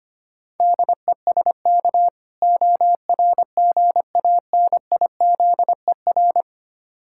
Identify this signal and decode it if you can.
Morse code, heard as a keyed tone, about 25 wpm, 700 Hz. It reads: DESK ORGANIZER
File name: practice / 25wpm / Difficult Words with Hints - 25wpm